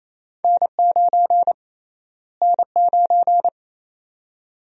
Morse code, heard as a keyed tone, about 28 wpm, 700 Hz. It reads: N9 N9
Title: USA Call Sign Prefixes - Repeated 2x - 28wpm